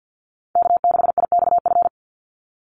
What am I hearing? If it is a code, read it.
Morse code, heard as a keyed tone, about 50 wpm, 700 Hz. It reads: K6IXF
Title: USA Call Signs 50wpm